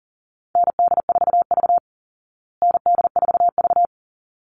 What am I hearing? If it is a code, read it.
Morse code, heard as a keyed tone, about 40 wpm, 700 Hz. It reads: ND4V ND4V
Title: Call Signs Repeated 3x - 40wpm